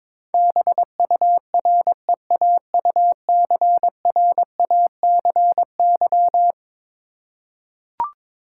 Morse code, heard as a keyed tone, about 21 wpm, 700 Hz. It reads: BUREAUCRACY
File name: practice / 22wpm / Eleven Letter Words - 22wpm